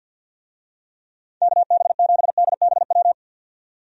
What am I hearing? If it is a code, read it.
Morse code, heard as a keyed tone, about 52 wpm, 700 Hz. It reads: KB7DBW
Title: USA Call Signs - 6 Characters - 50wpm